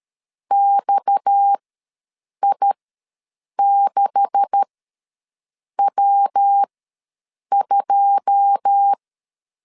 Morse code, heard as a keyed tone, about 13 wpm, spaced as for 7 wpm, 800 Hz. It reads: XI6W2
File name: LEZIONI / 9 286 ws 12wpm 3351